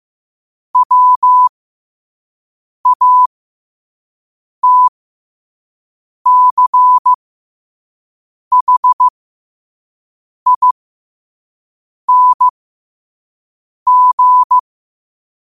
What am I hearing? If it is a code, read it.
Morse code, heard as a keyed tone, about 15 wpm, spaced as for 5 wpm, 1000 Hz. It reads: WATCHING
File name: qotdfiles / 5wpm